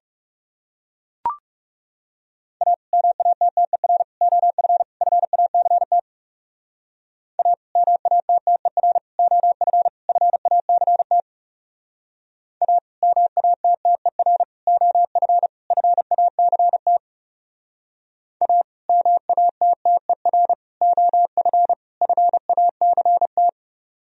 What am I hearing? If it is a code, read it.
Morse code, heard as a keyed tone, about 36 wpm, 700 Hz. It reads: A MATTER OF FACT A MATTER OF FACT A MATTER OF FACT A MATTER OF FACT
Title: Speed-Racing - Sets of 4 Words 30wpm